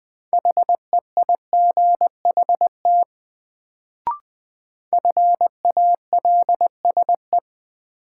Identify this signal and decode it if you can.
Morse code, heard as a keyed tone, about 20 wpm, 700 Hz. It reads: HEIGHT FALSE
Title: Sentences from Top 1600 Words - New Word Drill Alt - 20wpm